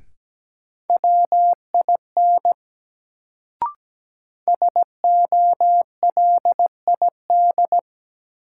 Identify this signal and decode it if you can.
Morse code, heard as a keyed tone, about 17 wpm, 700 Hz. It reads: WIN SOLID
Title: Sentences from Top 1500 Words - New Word Drill - 17wpm